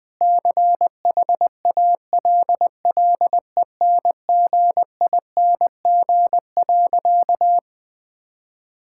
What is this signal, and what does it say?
Morse code, heard as a keyed tone, about 19 wpm, 700 Hz. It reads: CHALLENGING.
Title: Sentences with 5-Word Phrases - 20wpm